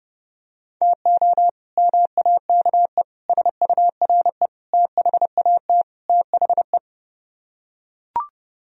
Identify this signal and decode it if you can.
Morse code, heard as a keyed tone, about 30 wpm, 700 Hz. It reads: TO MAKE SURE THAT THE